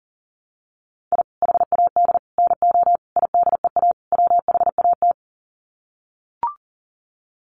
Morse code, heard as a keyed tone, about 40 wpm, 700 Hz. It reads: I HAD NO IDEA WHAT
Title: Sets of 5 Words Encore 40wpm